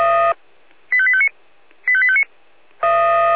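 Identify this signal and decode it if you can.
Five-tone selective call (ZVEI), heard as quick runs of five tones, tones of about 70 ms. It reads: 85859 85859
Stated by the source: Five-tone ZVEI selective call 85820 received on an SDR.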